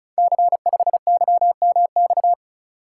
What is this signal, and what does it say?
Morse code, heard as a keyed tone, about 36 wpm, 700 Hz. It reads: C5YMX
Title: Code Groups - 5 Letter:Numbers - 35wpm